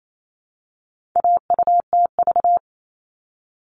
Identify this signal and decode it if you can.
Morse code, heard as a keyed tone, about 28 wpm, 700 Hz. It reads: AUTV
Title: All Cut Numbers - 4-Digit 28wpm